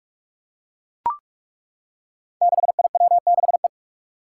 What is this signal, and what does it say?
Morse code, heard as a keyed tone, about 42 wpm, 700 Hz. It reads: 6IW6E
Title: Five Challenging Letter-Number Combinations 45wpm